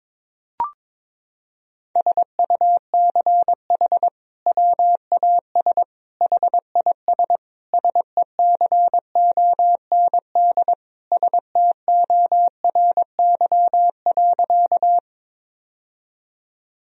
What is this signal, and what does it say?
Morse code, heard as a keyed tone, about 21 wpm, 700 Hz. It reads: SUCH WAS HIS SECOND STORY.
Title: Sentences from Top 1000 Words - Review Part 1 - 22wpm